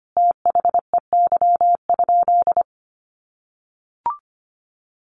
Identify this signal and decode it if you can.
Morse code, heard as a keyed tone, about 25 wpm, 700 Hz. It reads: THEY?
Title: Sentences from Top 300 Words 25wpm